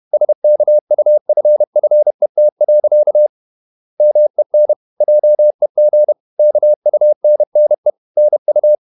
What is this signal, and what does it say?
Morse code, heard as a keyed tone, about 31 wpm, 600 Hz. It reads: SKUFFET. MEN JEG KUNNE NU
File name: Ferie QRQ 31wpm